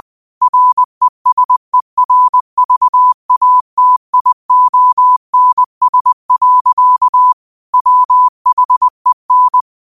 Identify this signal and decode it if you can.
Morse code, heard as a keyed tone, about 20 wpm, 1000 Hz. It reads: RESERVATIONS. WHEN
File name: qotdfiles / 20wpm